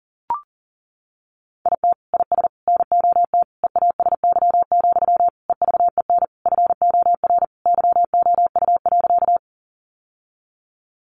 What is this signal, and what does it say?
Morse code, heard as a keyed tone, about 37 wpm, 700 Hz. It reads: IT IS NOT EASY, EVEN FOR YOU.
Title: Sentences from Top 700 Words 40wpm